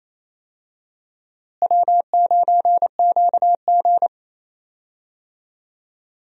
Morse code, heard as a keyed tone, about 28 wpm, 700 Hz. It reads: W9QG
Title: USA Call Signs - 4 Characters - Repeated 2x - 28wpm